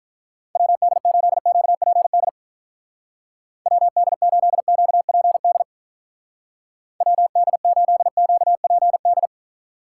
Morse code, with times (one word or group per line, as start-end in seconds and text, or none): 0.6-2.3 WD8QPD
3.7-5.6 WD8QPD
7.0-9.3 WD8QPD